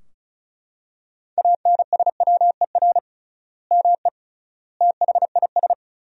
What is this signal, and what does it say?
Morse code, heard as a keyed tone, about 34 wpm, 700 Hz. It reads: ANSWER ME THIS
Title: 3x Word-Spacing - Sentences from Top 400 Words - 35wpm